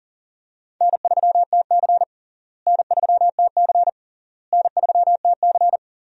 Morse code, heard as a keyed tone, about 40 wpm, 700 Hz. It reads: N3TC N3TC N3TC
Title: USA Call Signs - 4 Characters - Repeated 3x - 40wpm